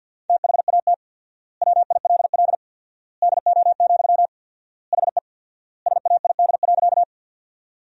Morse code, heard as a keyed tone, about 50 wpm, 700 Hz. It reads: THAT WILL DO, HE SAID.
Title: 4x Word-Spacing - Sentences from Top 100 Words - 50wpm